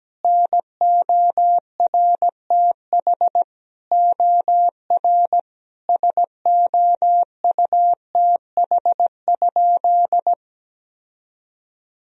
Morse code, heard as a keyed tone, about 17 wpm, 700 Hz. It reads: NORTH OR SOUTH?
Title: Sentences from Top 1000 Words - Review Part 2 - 17wpm